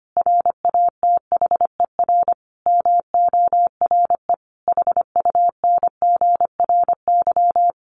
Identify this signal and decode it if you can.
Morse code, heard as a keyed tone, about 25 wpm, 700 Hz. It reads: RATHER MORE HUNGRY
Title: Sentences from Top 1400 Words 25wpm